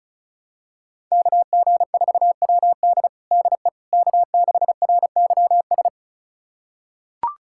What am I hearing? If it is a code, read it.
Morse code, heard as a keyed tone, about 36 wpm, 700 Hz. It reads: KG4WD DE K6RYS
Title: USA Callsign DE Callsign - 5 Characters - 35wpm